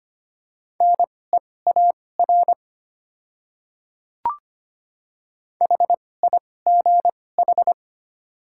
Morse code, heard as a keyed tone, about 25 wpm, spaced as for 18 wpm, 700 Hz. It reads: NEAR HIGH
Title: 2x Character Spacing - Top 300 Words - 25wpm